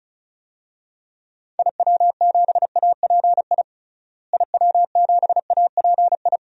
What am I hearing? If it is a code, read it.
Morse code, heard as a keyed tone, about 34 wpm, 700 Hz. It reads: IW7API IW7API
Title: DXCC-100 Call Signs - 6 Characters - Repeated 3x - 35wpm